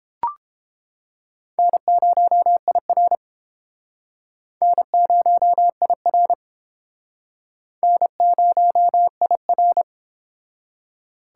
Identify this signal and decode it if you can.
Morse code, heard as a keyed tone, about 30 wpm, 700 Hz. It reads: N0IR N0IR N0IR